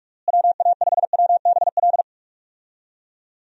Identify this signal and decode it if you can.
Morse code, heard as a keyed tone, about 41 wpm, 700 Hz. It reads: WA5WBL
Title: USA Call Signs - 6 Characters - Repeated 3x - 45wpm